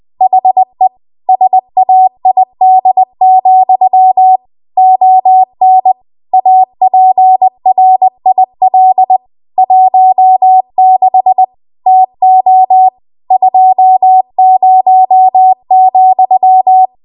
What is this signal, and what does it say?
Morse code, heard as a keyed tone, about 20 wpm, 750 Hz. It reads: HE SAID, ON APRIL 16 TO 20,
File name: Morse / 20WPM